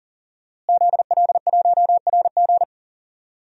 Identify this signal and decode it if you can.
Morse code, heard as a keyed tone, about 37 wpm, 700 Hz. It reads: ZL1RG
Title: DXCC-100 Call Signs - 5 Characters - Repeated 2x - 40wpm